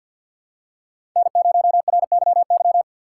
Morse code, heard as a keyed tone, about 52 wpm, 700 Hz. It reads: N0RYY
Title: USA Call Signs - 3x Character Spacing - 50wpm